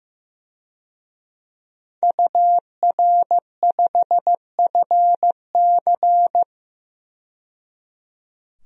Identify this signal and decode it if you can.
Morse code, heard as a keyed tone, about 15 wpm, 700 Hz. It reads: UR5FC
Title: DXCC-100 Call Signs - 5 Characters - 15wpm